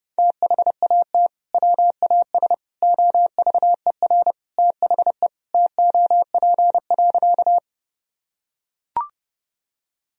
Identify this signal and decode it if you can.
Morse code, heard as a keyed tone, about 28 wpm, 700 Hz. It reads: THAT WAS OVER THE TOP.
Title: Sentences from Top 1000 Words - Review Part 2 - 30wpm